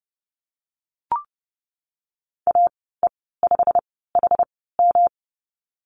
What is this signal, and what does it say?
Morse code, heard as a keyed tone, about 30 wpm, spaced as for 17 wpm, 700 Hz. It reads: AE5HM
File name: course / 30:10wpm / 116 Lesson 19g - Letter w - Call Signs 30wpm